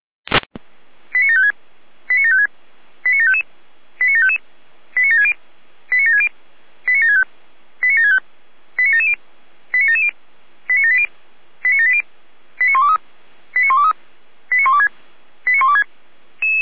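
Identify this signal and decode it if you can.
Five-tone selective call (ZVEI), heard as quick runs of five tones, tones of about 80 ms. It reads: 89756 89756 89755 89755 89760 89760 89765 89765 89770 89770 89780 89780 89123 89123 89126 89126
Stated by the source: ZVEI five-tone selective call 89751, received by SDR.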